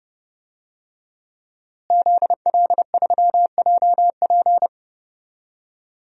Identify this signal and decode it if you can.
Morse code, heard as a keyed tone, about 31 wpm, 700 Hz. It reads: ZL3JP
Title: DXCC-100 Call Signs - 5 Characters - Repeated 2x - 30wpm